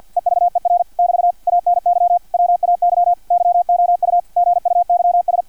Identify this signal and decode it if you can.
Morse code, heard as a keyed tone, about 50 wpm, 700 Hz, with radio noise. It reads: EVER <BT> ANY WAY YOU GUYS
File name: archived / CRSnet